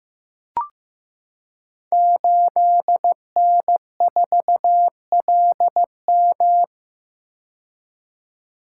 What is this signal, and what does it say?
Morse code, heard as a keyed tone, about 15 wpm, 700 Hz. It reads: 8N4LM
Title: Code Groups - 5 Letter:Numbers - 15wpm